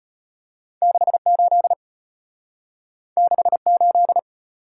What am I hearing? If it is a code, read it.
Morse code, heard as a keyed tone, about 32 wpm, 700 Hz. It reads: 68 68